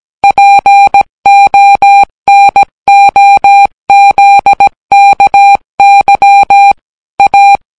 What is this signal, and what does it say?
Morse code, heard as a keyed tone, about 17 wpm, 800 Hz. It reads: PONOZKY A